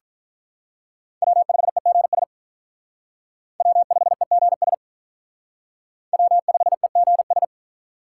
Morse code, heard as a keyed tone, about 46 wpm, 700 Hz. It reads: W5EGS W5EGS W5EGS